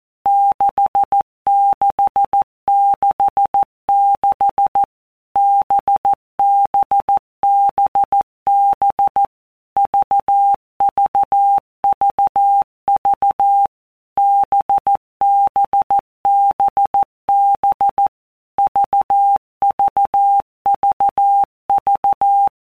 Morse code, heard as a keyed tone, about 14 wpm, 800 Hz. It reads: 6666 BBBB VVVV BBBB VVVV
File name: Basic Letter Groups By Blocks / Groups@13 / VB46@13